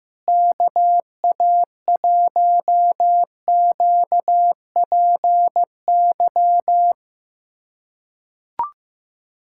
Morse code, heard as a keyed tone, about 15 wpm, 700 Hz. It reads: KA1QPY
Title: USA Callsign DE Callsign 15wpm